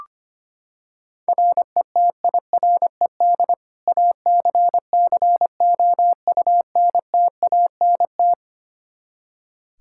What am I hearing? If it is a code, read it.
Morse code, heard as a keyed tone, about 25 wpm, 700 Hz. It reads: RETIRED ACCOUNTANT
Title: Practice QSO - Exchange 3 - More Info - 25wpm